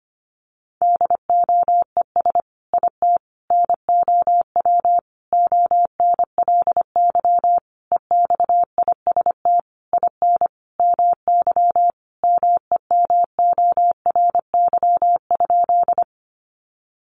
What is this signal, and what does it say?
Morse code, heard as a keyed tone, about 25 wpm, 700 Hz. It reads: DOES IT NOW ONLY EXIST IN MY MEMORY?
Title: Sentences from Top 1300 Words 25wpm